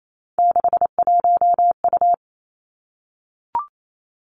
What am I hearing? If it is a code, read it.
Morse code, heard as a keyed tone, about 28 wpm, 700 Hz. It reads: 61U
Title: Three Challenging Letter-Number Combinations 28wpm